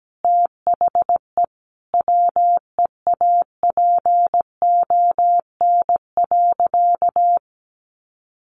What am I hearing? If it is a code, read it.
Morse code, heard as a keyed tone, about 17 wpm, 700 Hz. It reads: THE WEAPON.
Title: A Sound Of Thunder 17wpm